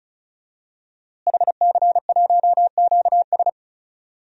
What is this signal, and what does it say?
Morse code, heard as a keyed tone, about 36 wpm, 700 Hz. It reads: HC1QS
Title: DXCC-100 Call Signs - 5 Characters - Repeated 3x - 35wpm